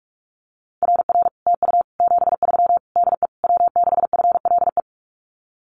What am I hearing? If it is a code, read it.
Morse code, heard as a keyed tone, about 46 wpm, 700 Hz. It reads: RR TU 73 DE W6FLE
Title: Common QSO Phrases Encore 45wpm